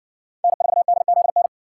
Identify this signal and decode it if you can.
Morse code, heard as a keyed tone, about 58 wpm, 700 Hz. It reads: N4DZN